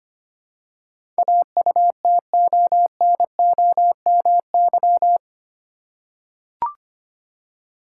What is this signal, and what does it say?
Morse code, heard as a keyed tone, about 25 wpm, 700 Hz. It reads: AUTONOMY
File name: Eight Letter Words - 25wpm